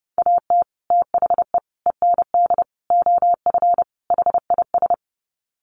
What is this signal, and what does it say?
Morse code, heard as a keyed tone, about 30 wpm, 700 Hz. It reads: AT THE END OF HIS
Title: Sets of 5 Words Encore 30wpm